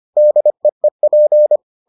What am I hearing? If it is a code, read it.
Morse code, heard as a keyed tone, about 25 wpm, 600 Hz. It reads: DEEP